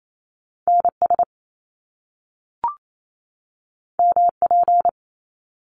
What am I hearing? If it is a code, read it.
Morse code, heard as a keyed tone, about 28 wpm, 700 Hz. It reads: NS MP
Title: Top Two Letter Combinations in Words 28wpm